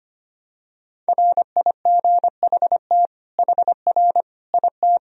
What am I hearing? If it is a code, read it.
Morse code, heard as a keyed tone, about 25 wpm, 700 Hz. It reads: RIGHT HR IT
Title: Sentences from Top 300 Words and Abbreviations 25wpm